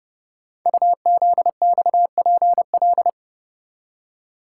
Morse code, heard as a keyed tone, about 28 wpm, 700 Hz. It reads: UZXPL
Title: Code Groups - 5 Letters - 30wpm